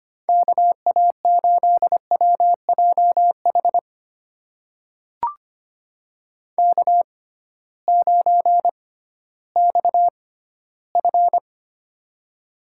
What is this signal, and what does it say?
Morse code, heard as a keyed tone, about 25 wpm, spaced as for 15 wpm, 700 Hz. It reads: KA8WJH K9XF
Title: USA Call Signs - 6x Character Spacing - 25wpm